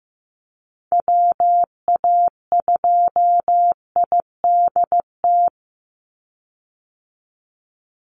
Morse code, heard as a keyed tone, about 15 wpm, 700 Hz. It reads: WA2IDT
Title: USA Call Signs 15wpm